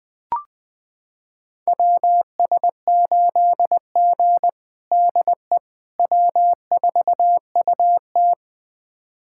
Morse code, heard as a keyed tone, about 20 wpm, 700 Hz. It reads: WS8G DE W4UT
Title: USA Callsign DE Callsign - 4 Characters - 20wpm